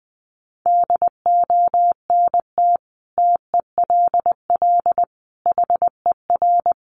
Morse code, heard as a keyed tone, about 20 wpm, 700 Hz. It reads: DONT TELL HER
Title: Sentences from Top 700 Words 20wpm